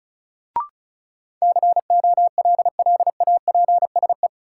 Morse code, heard as a keyed tone, about 36 wpm, 700 Hz. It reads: COLLAPSE
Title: Sets of 5 Words Encore - New Word Drill - 35wpm